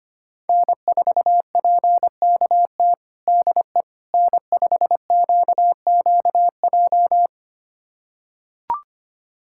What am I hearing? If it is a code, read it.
Morse code, heard as a keyed tone, about 25 wpm, 700 Hz. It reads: N4PKT DE N5QQJ
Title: USA Callsign DE Callsign - 5 Characters - 25wpm